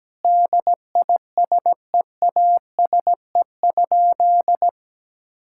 Morse code, heard as a keyed tone, about 17 wpm, 700 Hz. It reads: DISEASE?
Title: Sentences from Top 1000 Words - Review Part 6 - 17wpm